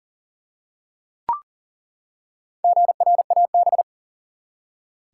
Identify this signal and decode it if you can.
Morse code, heard as a keyed tone, about 40 wpm, 700 Hz. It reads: GRAB
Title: Four Letter Words - 40wpm